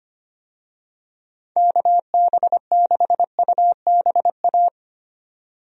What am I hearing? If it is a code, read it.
Morse code, heard as a keyed tone, about 25 wpm, 700 Hz. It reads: KB6UBA
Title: USA Call Signs - 6 Characters - 25wpm